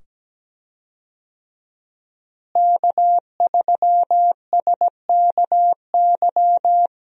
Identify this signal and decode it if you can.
Morse code, heard as a keyed tone, about 17 wpm, 700 Hz. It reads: K3SKY
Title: USA Call Signs - 5 Characters - Repeated 2x - 17wpm